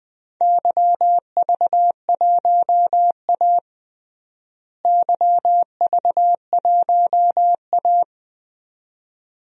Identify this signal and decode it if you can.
Morse code, heard as a keyed tone, about 20 wpm, 700 Hz. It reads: YV1A YV1A
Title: DXCC-100 Call Signs - 4 Characters - Repeated 2x - 20wpm